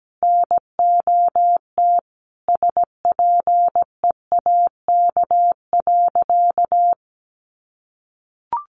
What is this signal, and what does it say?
Morse code, heard as a keyed tone, about 17 wpm, 700 Hz. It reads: NOT SPEAK.
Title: Sentences from Top 1100 Words 17wpm